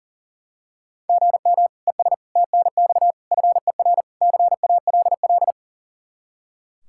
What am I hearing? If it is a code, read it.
Morse code, heard as a keyed tone, about 40 wpm, 700 Hz. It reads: GM ES TNX FER CALL
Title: Practice QSO - Exchange 1 - Beginning - 40wpm